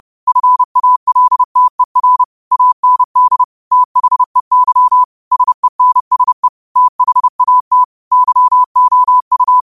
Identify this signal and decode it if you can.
Morse code, heard as a keyed tone, about 30 wpm, 1000 Hz. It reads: FALTER AND THEY SENSE THAT YOU
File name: qotdfiles / 30wpm